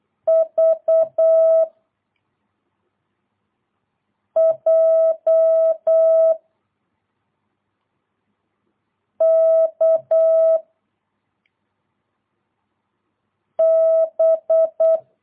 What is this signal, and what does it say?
Morse code, heard as a keyed tone, about 8 wpm, 660 Hz. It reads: V J K B
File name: jh8cbh / 2022cw